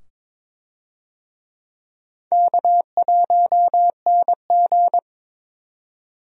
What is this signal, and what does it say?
Morse code, heard as a keyed tone, about 22 wpm, 700 Hz. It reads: K1NG
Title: USA Call Signs - 4 Characters - 22wpm